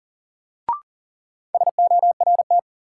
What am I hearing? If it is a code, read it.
Morse code, heard as a keyed tone, about 40 wpm, 700 Hz. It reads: SORT
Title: Sentences from Top 1300 Words - New Word Drill Alt - 40wpm